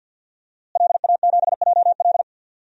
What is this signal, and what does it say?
Morse code, heard as a keyed tone, about 52 wpm, 700 Hz. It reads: LA7JL
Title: DXCC-100 Call Signs - 5 Characters - Repeated 3x - 50wpm